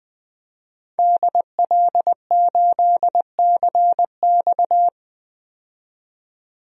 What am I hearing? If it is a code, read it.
Morse code, heard as a keyed tone, about 21 wpm, 700 Hz. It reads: DL8CX
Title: DXCC-100 Call Signs - 5 Characters - 20wpm